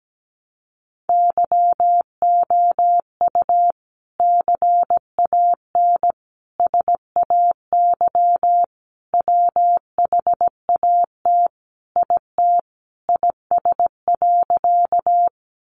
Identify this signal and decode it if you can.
Morse code, heard as a keyed tone, about 17 wpm, 700 Hz. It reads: YOU CAN SAY WHAT IT IS.
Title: Sentences from Top 200 Words 17wpm